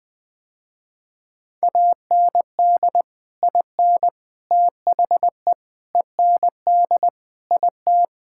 Morse code, heard as a keyed tone, about 20 wpm, 700 Hz. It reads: AND IN THE END IT
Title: Sentences with 3-Word Phrases - Spoken Beginning - 20wpm